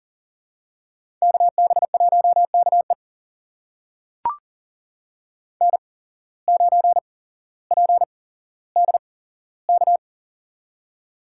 Morse code, heard as a keyed tone, about 40 wpm, spaced as for 17 wpm, 700 Hz. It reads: KB1KE N9PDK
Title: USA Call Signs - 5 Characters - 8x Character Spacing - 40wpm